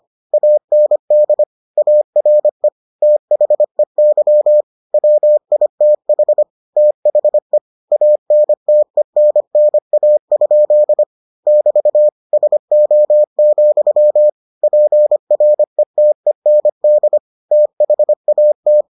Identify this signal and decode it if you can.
Morse code, heard as a keyed tone, about 25 wpm, 600 Hz. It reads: AND ARE THEY WITH THE ANTENNA? <BT> SO, PRETEND THAT